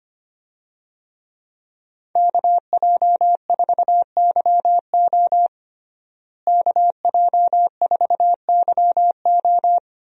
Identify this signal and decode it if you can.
Morse code, heard as a keyed tone, about 25 wpm, 700 Hz. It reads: KJ4YO KJ4YO
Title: USA Call Signs - 5 Characters - Repeated 2x - 25wpm